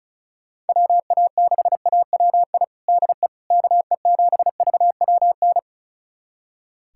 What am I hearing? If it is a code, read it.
Morse code, heard as a keyed tone, about 36 wpm, 690 Hz. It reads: WA6AWI DE KE7VWN
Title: USA Callsign DE Callsign - 6 Characters - 35wpm